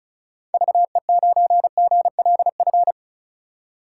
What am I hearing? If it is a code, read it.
Morse code, heard as a keyed tone, about 36 wpm, 700 Hz. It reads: VE9GLF
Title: K1USN SST QSO Practice - 35wpm